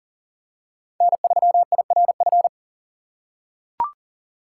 Morse code, heard as a keyed tone, about 40 wpm, 700 Hz. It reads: N3IRF
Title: USA Call Signs - 5 Characters - 6x Character Spacing - 40wpm